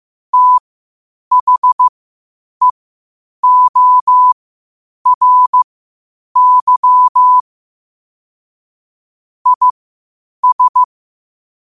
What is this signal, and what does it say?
Morse code, heard as a keyed tone, about 15 wpm, spaced as for 8 wpm, 1000 Hz. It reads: THEORY IS
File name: qotdfiles / 8wpm